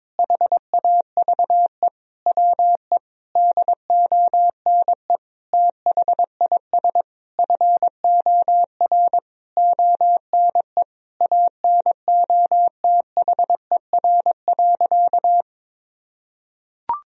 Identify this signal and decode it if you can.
Morse code, heard as a keyed tone, about 22 wpm, 700 Hz. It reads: HAVE WE DONE THIS FOR ONE ANOTHER.
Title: Sentences from Top 1000 Words - Review Part 1 - 22wpm